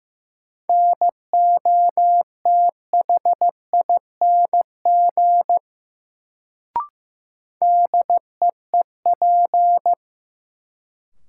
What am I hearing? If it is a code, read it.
Morse code, heard as a keyed tone, about 15 wpm, 700 Hz. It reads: NOTHING DEEP
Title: Sentences from Top 500 Words - New Word Drill - 15wpm